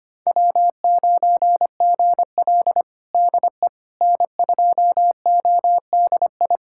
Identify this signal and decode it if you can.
Morse code, heard as a keyed tone, about 25 wpm, 700 Hz. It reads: W9GL DE N2ODI
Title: Practice QSO - Exchange 4 - Ending - 25wpm